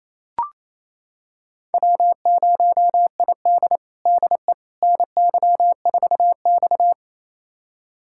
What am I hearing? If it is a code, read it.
Morse code, heard as a keyed tone, about 28 wpm, 700 Hz. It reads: W0ID DE NY4X
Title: USA Callsign DE Callsign - 4 Characters - 28wpm